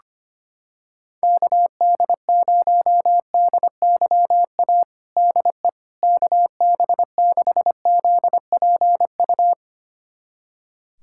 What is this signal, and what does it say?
Morse code, heard as a keyed tone, about 25 wpm, 700 Hz. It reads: KD0DYA DE KB6ZPU
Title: USA Callsign DE Callsign 25wpm